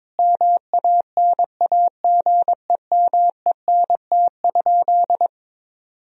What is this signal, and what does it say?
Morse code, heard as a keyed tone, about 21 wpm, 700 Hz. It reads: MANAGEMENT?
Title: Sentences from Top 1000 Words - Review Part 4 - 22wpm